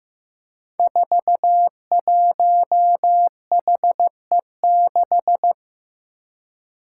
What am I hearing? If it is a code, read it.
Morse code, heard as a keyed tone, about 15 wpm, 700 Hz. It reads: 41HE6
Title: Five Challenging Letter-Number Combinations 15wpm